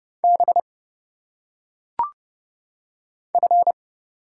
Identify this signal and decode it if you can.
Morse code, heard as a keyed tone, about 29 wpm, 700 Hz. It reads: B F
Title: Single Letters - Phonetic Pronunciation 30wpm